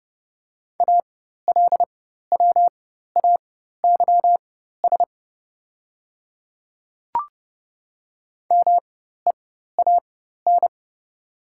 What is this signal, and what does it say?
Morse code, heard as a keyed tone, about 30 wpm, spaced as for 14 wpm, 700 Hz. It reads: ALWAYS MEAN
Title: 4x Character Spacing - Top 300 Words - 30wpm